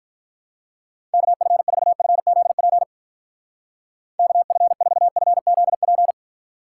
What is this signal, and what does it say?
Morse code, heard as a keyed tone, about 46 wpm, 700 Hz. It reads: KF4FZP KF4FZP